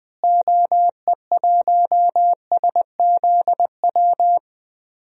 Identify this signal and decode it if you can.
Morse code, heard as a keyed tone, about 21 wpm, 700 Hz. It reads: OE1SZW
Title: DXCC-100 Call Signs - 6 Characters - Repeated 3x - 20wpm